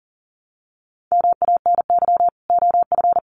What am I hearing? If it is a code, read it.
Morse code, heard as a keyed tone, about 39 wpm, 700 Hz. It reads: MANY OF
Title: Sets of 2 Words 40wpm